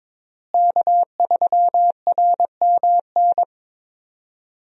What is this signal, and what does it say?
Morse code, heard as a keyed tone, about 22 wpm, 700 Hz. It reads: K3RMN
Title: USA Call Signs - 5 Characters - Repeated 3x - 22wpm